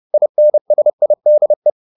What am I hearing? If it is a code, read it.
Morse code, heard as a keyed tone, about 30 wpm, 600 Hz. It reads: INSIDE